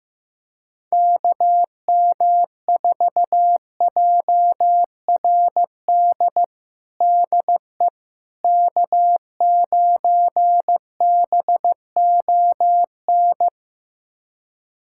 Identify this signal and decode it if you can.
Morse code, heard as a keyed tone, about 16 wpm, 700 Hz. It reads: KM4JRD DE K9BON
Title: USA Callsign DE Callsign 15wpm